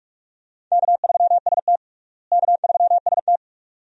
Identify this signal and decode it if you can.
Morse code, heard as a keyed tone, about 46 wpm, 700 Hz. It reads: K3ST K3ST
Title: USA Call Signs - 4 Characters - Repeated 2x - 45wpm